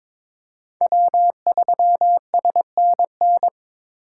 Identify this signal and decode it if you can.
Morse code, heard as a keyed tone, about 26 wpm, 700 Hz. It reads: W3SNN
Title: Speed Racing - USA 1x3 Call Signs 22wpm